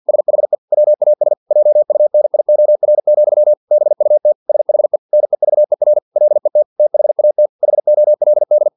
Morse code, heard as a keyed tone, about 50 wpm, 600 Hz. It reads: SHE WAS JUNIOR, BUT SHE NEVER LET THAT HOLD
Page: Never 49wpm